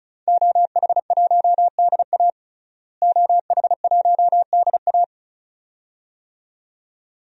Morse code, heard as a keyed tone, about 36 wpm, 700 Hz. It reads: OH1DA OH1DA
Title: DXCC-100 Call Signs - 5 Characters - Repeated 2x - 35wpm